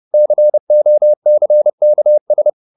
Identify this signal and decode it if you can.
Morse code, heard as a keyed tone, about 30 wpm, 600 Hz. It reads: COCKS
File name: the old man and the sea / the old man and the sea 1 30wpm